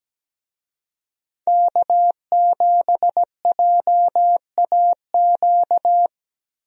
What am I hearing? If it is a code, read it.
Morse code, heard as a keyed tone, about 17 wpm, 700 Hz. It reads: K7JAQ
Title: USA Call Signs - 5 Characters - Repeated 2x - 17wpm